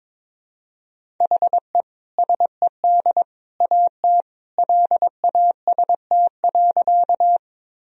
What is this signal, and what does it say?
Morse code, heard as a keyed tone, about 21 wpm, 700 Hz. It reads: HE SED AT LAST.
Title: Sentences from Top 300 Words and Abbreviations 22wpm